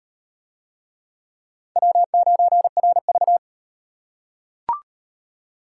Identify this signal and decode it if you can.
Morse code, heard as a keyed tone, about 38 wpm, 700 Hz. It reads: W9RV